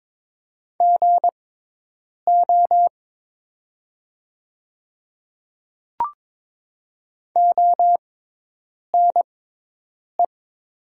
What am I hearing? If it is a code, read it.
Morse code, heard as a keyed tone, about 22 wpm, spaced as for 8 wpm, 700 Hz. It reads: GO ONE
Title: 6x Character Spacing - Top 100 Words - 22wpm